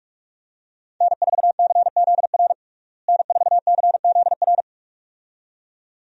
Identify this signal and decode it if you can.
Morse code, heard as a keyed tone, about 41 wpm, 700 Hz. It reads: N4CZR N4CZR
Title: USA Call Signs - 5 Characters - Repeated 2x - 45wpm